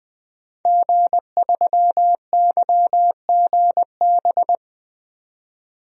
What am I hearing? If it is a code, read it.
Morse code, heard as a keyed tone, about 20 wpm, 700 Hz. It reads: G3YGB
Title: DXCC-100 Call Signs - 5 Characters - 20wpm